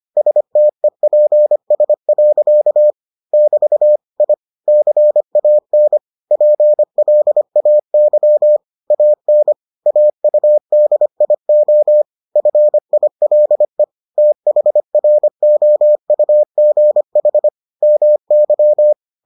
Morse code, heard as a keyed tone, about 25 wpm, 600 Hz. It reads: STEPS. <BT> I CAN PLAY AN AUDIO FILE THROUGH MY